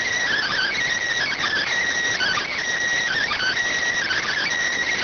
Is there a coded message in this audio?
Yes: Emergency Alert System (SAME) tones from 0.3 to 4.4 s, heard as harsh buzzing bursts.